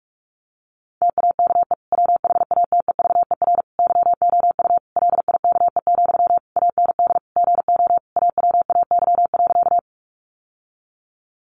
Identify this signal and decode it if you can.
Morse code, heard as a keyed tone, about 44 wpm, 700 Hz. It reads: TAKE WHATEVER YOU LIKE, AND GO AWAY.
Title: Sentences from Top 1100 Words 45wpm